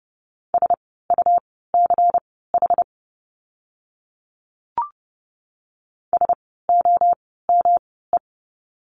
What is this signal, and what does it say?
Morse code, heard as a keyed tone, about 30 wpm, spaced as for 17 wpm, 700 Hz. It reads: SUCH SOME
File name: course / 30:10wpm / 237 Lesson 38f - Letter q - Words 30wpm